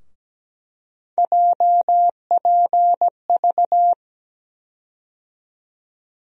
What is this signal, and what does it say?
Morse code, heard as a keyed tone, about 17 wpm, 700 Hz. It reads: JPV
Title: USA Call Sign Postfixes - 3 Letters 17wpm